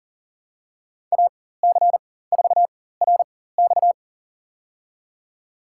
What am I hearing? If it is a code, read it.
Morse code, heard as a keyed tone, about 40 wpm, spaced as for 19 wpm, 700 Hz. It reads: AC4RX
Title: USA Call Signs - 5 Characters - 4x Character Spacing - 40wpm